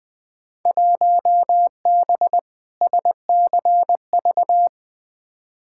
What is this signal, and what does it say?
Morse code, heard as a keyed tone, about 20 wpm, 700 Hz. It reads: 1B SCV
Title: Field Day Exchanges - 20wpm